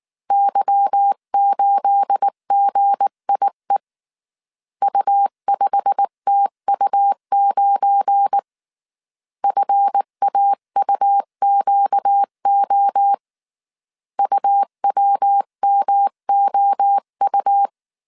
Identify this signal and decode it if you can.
Morse code, heard as a keyed tone, about 19 wpm, 800 Hz. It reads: Y8GIE U5TU9 FAUQO UWMOU
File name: CW / DET / 1 9 1500C S B 18wpm 2551